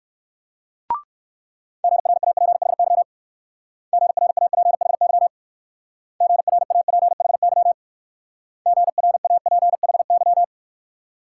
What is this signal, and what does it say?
Morse code, heard as a keyed tone, about 55 wpm, 700 Hz. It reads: GRAPHY GRAPHY GRAPHY GRAPHY
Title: Speed-Racing Multi-letter Phonemes - 45wpm